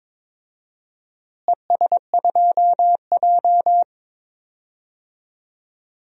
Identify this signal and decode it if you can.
Morse code, heard as a keyed tone, about 21 wpm, 700 Hz. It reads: ES2J
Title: DXCC-100 Call Signs - 4 Characters - Repeated 2x - 22wpm